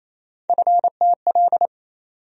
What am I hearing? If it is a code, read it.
Morse code, heard as a keyed tone, about 28 wpm, 700 Hz. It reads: FTL
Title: USA Call Sign Postfixes - 3 Letters 28wpm